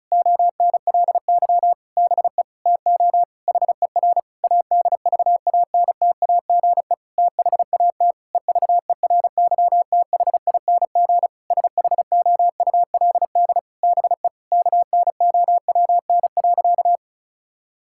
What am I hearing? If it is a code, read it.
Morse code, heard as a keyed tone, about 36 wpm, 700 Hz. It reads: ONLY BE TO HER ADVANTAGE THAT EVERYTHING SHOULD BE KNOWN.
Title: Sentences from Top 1000 Words - Review Part 6 - 35wpm